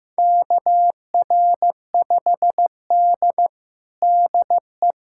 Morse code, heard as a keyed tone, about 15 wpm, 700 Hz. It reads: KR5D DE
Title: USA Callsign DE Callsign - 4 Characters - 15wpm